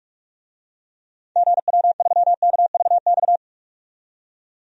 Morse code, heard as a keyed tone, about 44 wpm, 700 Hz. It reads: GW3KVX
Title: DXCC-100 Call Signs - 6 Characters - 45wpm